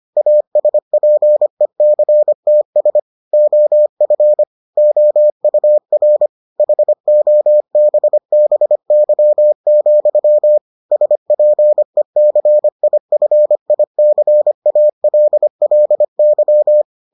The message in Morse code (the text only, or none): ASPECTS OF OUR HOBBY, SPECIFICALLY